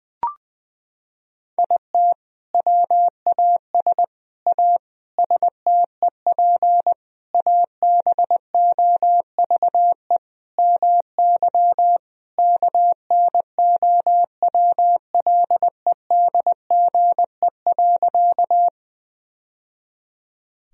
Morse code, heard as a keyed tone, about 20 wpm, 700 Hz. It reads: IT WAS A STEP ABOVE MY KNOWLEDGE.
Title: Sentences from Top 1000 Words - Review Part 3 - 20wpm